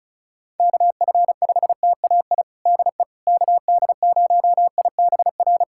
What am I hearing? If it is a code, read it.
Morse code, heard as a keyed tone, about 36 wpm, 700 Hz. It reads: KF5TAI DE KD0IBR
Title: USA Callsign DE Callsign 35wpm